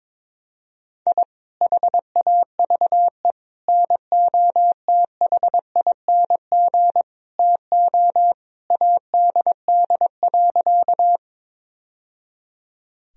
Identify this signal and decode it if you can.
Morse code, heard as a keyed tone, about 22 wpm, 700 Hz. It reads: I HAVE NOTHING TO ADD.
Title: Sentences from Top 1000 Words - Review Part 1 - 22wpm